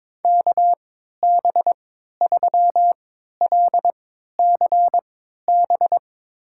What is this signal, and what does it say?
Morse code, heard as a keyed tone, about 22 wpm, spaced as for 12 wpm, 700 Hz. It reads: KB3LCB
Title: USA Call Signs - 6 Characters - 3x Character Spacing - 22wpm